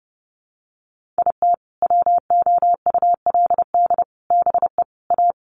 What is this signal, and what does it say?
Morse code, heard as a keyed tone, about 30 wpm, 700 Hz. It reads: IT WOULD BE A
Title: Sets of 4 Words 30wpm